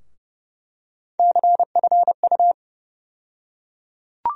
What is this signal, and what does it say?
Morse code, heard as a keyed tone, about 31 wpm, 700 Hz. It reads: CFU
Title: USA Call Sign Postfixes - 3 Letters 30wpm